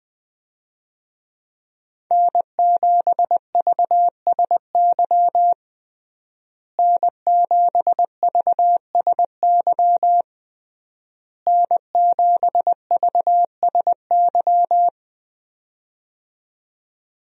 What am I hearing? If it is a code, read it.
Morse code, heard as a keyed tone, about 20 wpm, 700 Hz. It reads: N7VSY N7VSY N7VSY
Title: USA Call Signs - 5 Characters - Repeated 3x - 20wpm